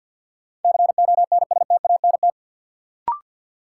Morse code, heard as a keyed tone, about 53 wpm, 700 Hz. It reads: CONSTANT
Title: Sentences from Top 1500 Words - New Word Drill Alt - 50wpm